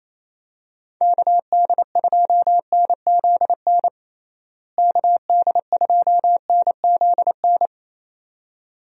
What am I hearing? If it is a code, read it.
Morse code, heard as a keyed tone, about 27 wpm, 700 Hz. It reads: KD2NZN KD2NZN
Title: USA Call Signs - 6 Characters - Repeated 2x - 28wpm